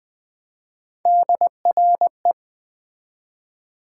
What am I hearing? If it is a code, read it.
Morse code, heard as a keyed tone, about 20 wpm, 700 Hz. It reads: DRE
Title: Build A Word - 1 Letter At A Time - 5-Letter Words - 20wpm